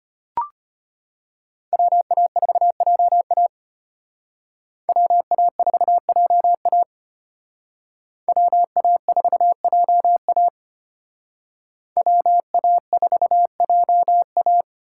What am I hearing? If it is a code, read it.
Morse code, heard as a keyed tone, about 31 wpm, 700 Hz. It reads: WA4JA WA4JA WA4JA WA4JA
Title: Speed Racing - USA Call Signs 25wpm